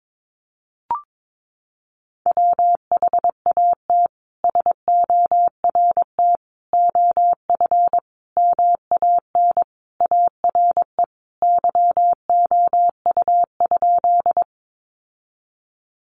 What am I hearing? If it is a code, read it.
Morse code, heard as a keyed tone, about 22 wpm, 700 Hz. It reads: WHAT SORT OF MAN ARE YOU?
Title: Sentences from Top 1300 Words 22wpm